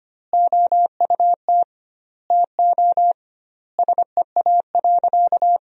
Morse code, heard as a keyed tone, about 25 wpm, 700 Hz. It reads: OUT TO SEA.
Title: 2x Word-Spacing - Sentences from Top 400 Words - 25wpm